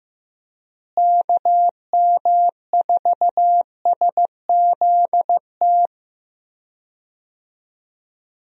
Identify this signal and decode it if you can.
Morse code, heard as a keyed tone, about 15 wpm, 700 Hz. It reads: KM4SZT
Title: USA Call Signs - 6 Characters - Repeated 3x - 15wpm